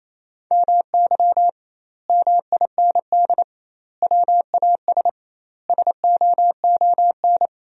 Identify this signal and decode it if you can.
Morse code, heard as a keyed tone, about 28 wpm, 700 Hz. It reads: MY MIND WAS SOON
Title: 2x Word-Spacing - Sentences from Top 500 Words - 28wpm